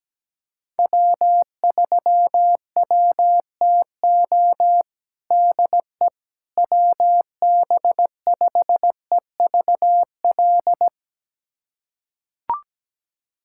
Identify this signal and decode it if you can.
Morse code, heard as a keyed tone, about 17 wpm, 700 Hz. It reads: W3WTO DE WB5EVL
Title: USA Callsign DE Callsign 17wpm